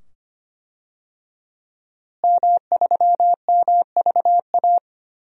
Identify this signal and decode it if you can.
Morse code, heard as a keyed tone, about 25 wpm, 700 Hz. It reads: M3MVA
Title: DXCC-100 Call Signs - 5 Characters - Repeated 2x - 25wpm